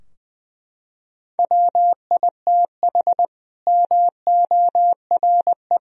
Morse code, heard as a keyed tone, about 20 wpm, 700 Hz. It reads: WITH MORE
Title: Sentences from Top 1000 Words - Review Part 3 - 20wpm